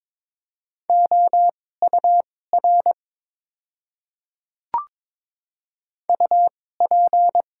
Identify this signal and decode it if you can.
Morse code, heard as a keyed tone, about 22 wpm, spaced as for 16 wpm, 700 Hz. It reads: OUR UP
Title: 2x Character Spacing - Top 500 Words - 22wpm